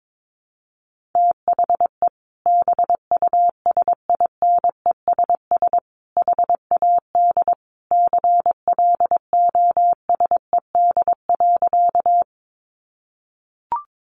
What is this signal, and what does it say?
Morse code, heard as a keyed tone, about 22 wpm, 700 Hz. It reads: THE BUSINESS HAD CLOSED.
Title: Sentences from Top 1400 Words 22wpm